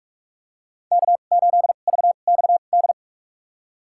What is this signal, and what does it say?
Morse code, heard as a keyed tone, about 46 wpm, spaced as for 32 wpm, 700 Hz. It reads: K8VXD
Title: USA Call Signs - 5 Characters - 2x Character Spacing - 45wpm